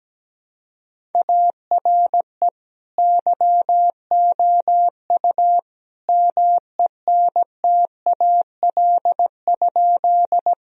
Morse code, heard as a keyed tone, about 17 wpm, 700 Hz. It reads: ARE YOU MENTAL?
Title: Sentences from Top 1000 Words - Review Part 4 - 17wpm